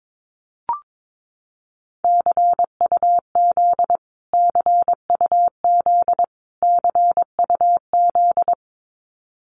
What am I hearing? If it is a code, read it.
Morse code, heard as a keyed tone, about 22 wpm, 700 Hz. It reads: CUZ CUZ CUZ
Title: Abbreviations Repeated 3x - 22wpm